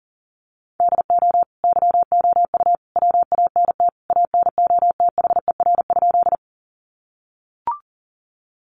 Morse code, heard as a keyed tone, about 40 wpm, 700 Hz. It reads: DO YOU WANT ANOTHER?
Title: Sentences from Top 600 Words 40wpm